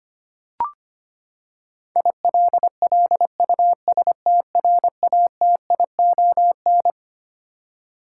Morse code, heard as a keyed tone, about 25 wpm, 700 Hz. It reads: ILLUSTRATION
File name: Twelve Letter Words - 25wpm